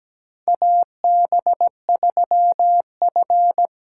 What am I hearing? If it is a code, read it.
Morse code, heard as a keyed tone, about 17 wpm, 700 Hz. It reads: AB3F
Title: USA Call Signs - 4 Characters - 2x Character Spacing - 17wpm